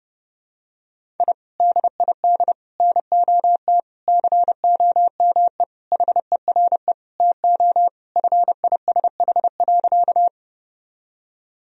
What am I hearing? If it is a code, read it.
Morse code, heard as a keyed tone, about 30 wpm, 700 Hz. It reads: I DID NOT COME HERE TO FISH.
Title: Sentences from Top 1000 Words - Review Part 1 - 30wpm